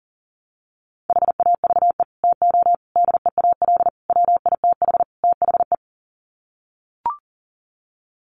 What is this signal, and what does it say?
Morse code, heard as a keyed tone, about 37 wpm, 700 Hz. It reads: HAVE TO DEAL WITH THE
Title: Sets of 5 Words Encore2 40wpm